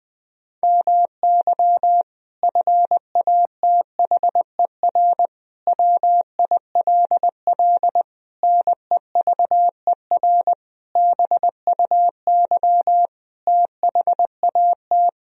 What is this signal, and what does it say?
Morse code, heard as a keyed tone, about 20 wpm, 700 Hz. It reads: MY FATHER WILL NEVER BUY THAT
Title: Sentences from Top 1000 Words - Review Part 4 - 20wpm